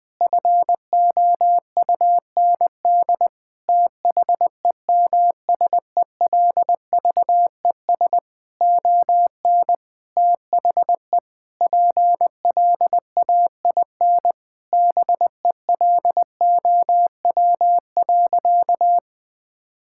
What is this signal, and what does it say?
Morse code, heard as a keyed tone, about 20 wpm, 700 Hz. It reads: FOUND THEMSELVES ON THE PLAIN BELOW.
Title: Sentences from Top 1000 Words - Review Part 5 - 20wpm